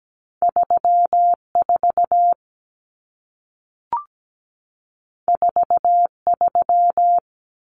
Morse code, heard as a keyed tone, about 17 wpm, 700 Hz. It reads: 34 43
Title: Numbers - 2-Digit 17wpm